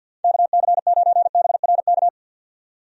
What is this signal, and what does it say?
Morse code, heard as a keyed tone, about 53 wpm, 700 Hz. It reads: KC9BRK
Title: USA Call Signs - 6 Characters - 50wpm